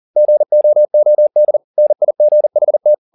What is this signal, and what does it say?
Morse code, heard as a keyed tone, about 40 wpm, 600 Hz. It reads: GOOD NIGHT